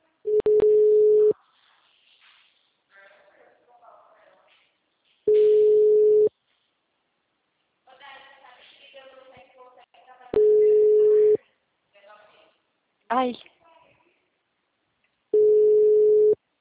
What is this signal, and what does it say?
Ringback tone (Europe, 425 Hz), with speech.